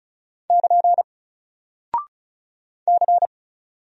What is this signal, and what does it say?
Morse code, heard as a keyed tone, about 36 wpm, 700 Hz. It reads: <KN> C
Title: Single Character (Letter:Number:Punctuation:Prosign) 35wpm